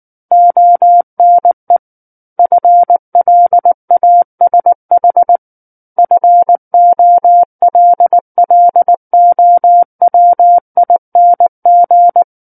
Morse code, heard as a keyed tone, about 19 wpm, 700 Hz. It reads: ONE FLASH FOLLOWING